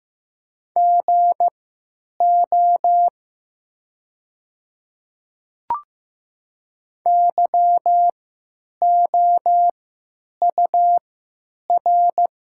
Morse code, heard as a keyed tone, about 15 wpm, spaced as for 9 wpm, 700 Hz. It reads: GO YOUR
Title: 3x Character Spacing - Top 100 Words - 15wpm